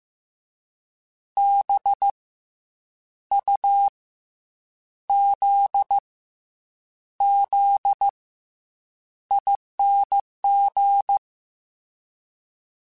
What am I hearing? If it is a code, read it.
Morse code, heard as a keyed tone, about 15 wpm, 800 Hz. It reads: B U Z Z ING
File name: Special INGs / Special INGs-mp3s / SlowToFaster ing @13